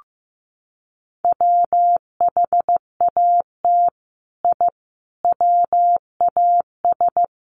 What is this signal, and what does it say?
Morse code, heard as a keyed tone, about 15 wpm, 700 Hz. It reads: WHAT I WAS
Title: Sets of 5 Words Encore2 15wpm